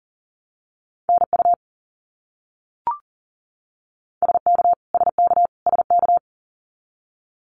Morse code, heard as a keyed tone, about 40 wpm, 700 Hz. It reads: NU SK SK SK
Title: Canadian Provinces and Territories - Repeated 3x - 40wpm